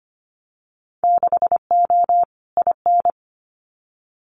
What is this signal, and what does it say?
Morse code, heard as a keyed tone, about 25 wpm, 700 Hz. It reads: 6O IN
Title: Winter Field Day Exchanges Encore 25wpm